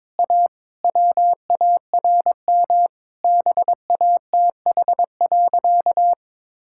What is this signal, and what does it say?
Morse code, heard as a keyed tone, about 22 wpm, 700 Hz. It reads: A WARM BATH.
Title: Sentences from Top 1600 Words 22wpm